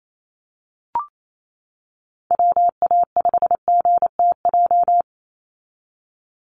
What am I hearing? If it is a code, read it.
Morse code, heard as a keyed tone, about 28 wpm, 700 Hz. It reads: WA5GTJ
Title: USA Call Signs 28wpm